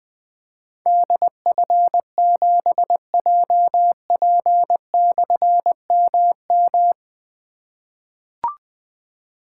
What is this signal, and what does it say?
Morse code, heard as a keyed tone, about 20 wpm, 700 Hz. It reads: DF7JP/MM
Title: Difficult Call Signs 20wpm